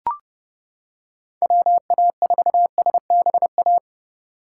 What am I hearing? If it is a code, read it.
Morse code, heard as a keyed tone, about 29 wpm, 700 Hz. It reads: WA4SBA